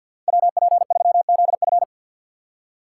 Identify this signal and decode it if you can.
Morse code, heard as a keyed tone, about 53 wpm, 700 Hz. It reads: WP3ZF
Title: USA Call Signs - 5 Characters - 2x Character Spacing - 50wpm